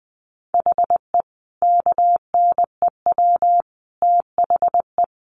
Morse code, heard as a keyed tone, about 20 wpm, 700 Hz. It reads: HE KNEW THE
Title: Sentences from Top 1300 Words 20wpm